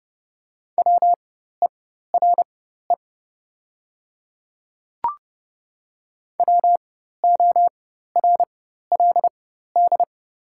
Morse code, heard as a keyed tone, about 30 wpm, spaced as for 14 wpm, 690 Hz. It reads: WERE WORLD
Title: 4x Character Spacing - Top 200 Words - 30wpm